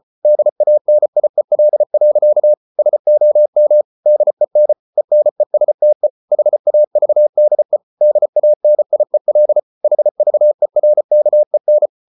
Morse code, heard as a keyed tone, about 34 wpm, 600 Hz. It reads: DANIEL. SOM DEN ENESTE HAVDE DANIEL HVERKEN
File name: Ferie QRQ 34wpm